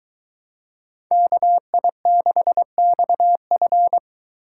Morse code, heard as a keyed tone, about 22 wpm, 700 Hz. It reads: KI6XF